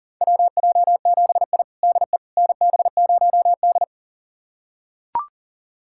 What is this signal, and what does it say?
Morse code, heard as a keyed tone, about 37 wpm, 700 Hz. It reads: WJ7I DE NB0D
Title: USA Callsign DE Callsign - 4 Characters - 40wpm